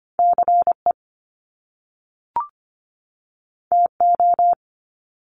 Morse code, heard as a keyed tone, about 25 wpm, 700 Hz. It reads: CE TO
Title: Top Two Letter Combinations in Words 25wpm